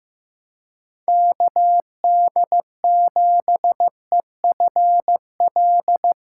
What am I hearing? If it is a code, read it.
Morse code, heard as a keyed tone, about 15 wpm, 700 Hz. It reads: KD7EFL
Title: USA Callsign DE Callsign - 6 Characters - 15wpm